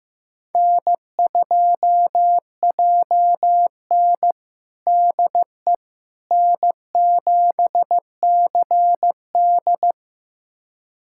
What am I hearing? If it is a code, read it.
Morse code, heard as a keyed tone, about 15 wpm, 700 Hz. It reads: N2JN DE N7CD
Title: USA Callsign DE Callsign - 4 Characters - 15wpm